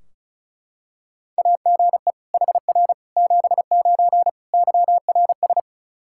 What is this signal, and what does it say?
Morse code, heard as a keyed tone, about 34 wpm, 700 Hz. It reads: AGE HR 79 YRS
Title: K1USN SST QSO Practice 35wpm